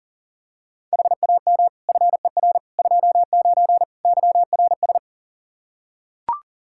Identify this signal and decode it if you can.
Morse code, heard as a keyed tone, about 40 wpm, 700 Hz. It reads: HAM FER 29 YRS
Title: Practice QSO - Exchange 3 - More Info - 40wpm